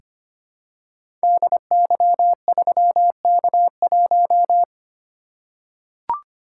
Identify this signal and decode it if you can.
Morse code, heard as a keyed tone, about 25 wpm, 700 Hz. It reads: DY3K1
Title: Code Groups - 5 Letter:Numbers - 25wpm